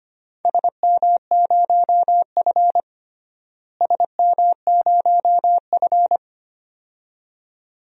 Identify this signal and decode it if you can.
Morse code, heard as a keyed tone, about 25 wpm, 700 Hz. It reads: SM0F SM0F
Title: DXCC-100 Call Signs - 4 Characters - Repeated 2x - 25wpm